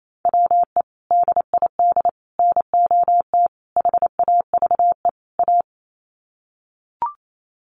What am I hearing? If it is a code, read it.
Morse code, heard as a keyed tone, about 28 wpm, 700 Hz. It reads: WE DID NOT HAVE A
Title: Sets of 5 Words Encore2 28wpm